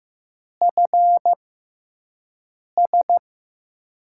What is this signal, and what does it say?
Morse code, heard as a keyed tone, about 15 wpm, spaced as for 5 wpm, 700 Hz. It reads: FS
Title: USA Call Signs - 6x Character Spacing - 15wpm